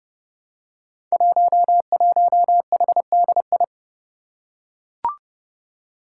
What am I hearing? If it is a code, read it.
Morse code, heard as a keyed tone, about 30 wpm, 700 Hz. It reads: 11HDI
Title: Five Challenging Letter-Number Combinations 30wpm